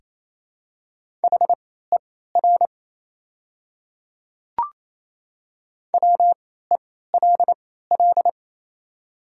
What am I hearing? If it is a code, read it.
Morse code, heard as a keyed tone, about 27 wpm, spaced as for 16 wpm, 700 Hz. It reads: HER WELL
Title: 3x Character Spacing - Top 200 Words - 28wpm